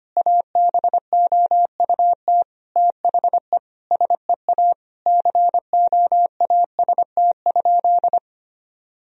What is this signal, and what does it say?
Morse code, heard as a keyed tone, about 25 wpm, 700 Hz. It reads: ABOUT THE SEA COAST?
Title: Sentences from Top 1000 Words - Review Part 4 - 25wpm